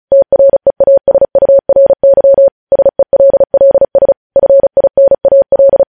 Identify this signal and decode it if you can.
Morse code, heard as a keyed tone, about 36 wpm, 550 Hz. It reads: TREASURY SELLS FINAL
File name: podcast / News headlines-1x-35x35